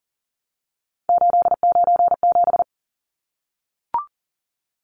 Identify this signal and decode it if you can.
Morse code, heard as a keyed tone, about 40 wpm, 700 Hz. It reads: 897
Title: Numbers - 3-Digit 40wpm